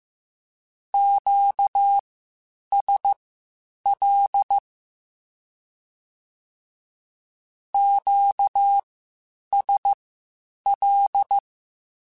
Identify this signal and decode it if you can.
Morse code, heard as a keyed tone, about 15 wpm, spaced as for 8 wpm, 800 Hz. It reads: QSL QSL
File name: (spaced) / Q-Calls Spaced-mp3s / Q Calls Twice sp @13